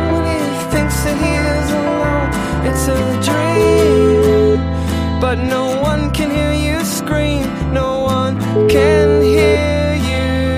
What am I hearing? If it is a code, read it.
Ringback tone (Europe, 425 Hz), with noise.